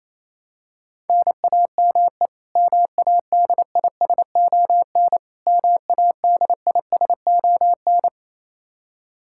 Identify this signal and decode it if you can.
Morse code, heard as a keyed tone, about 28 wpm, 700 Hz. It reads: NAME MADISON MADISON
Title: K1USN SST QSO Practice 28wpm